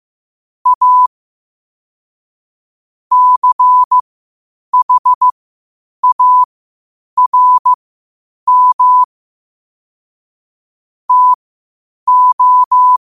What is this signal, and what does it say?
Morse code, heard as a keyed tone, about 15 wpm, spaced as for 8 wpm, 1000 Hz. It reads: A CHARM TO
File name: 8wpm